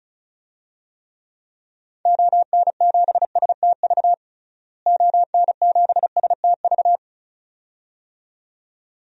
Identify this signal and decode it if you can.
Morse code, heard as a keyed tone, about 36 wpm, 700 Hz. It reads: ON7STV ON7STV
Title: DXCC-100 Call Signs - 6 Characters - Repeated 2x - 35wpm